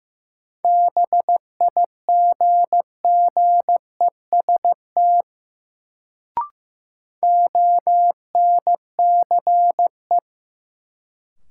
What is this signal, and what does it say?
Morse code, heard as a keyed tone, about 15 wpm, 700 Hz. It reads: BIGGEST ONCE
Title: Sets of 4 Words - New Word Drill - 15wpm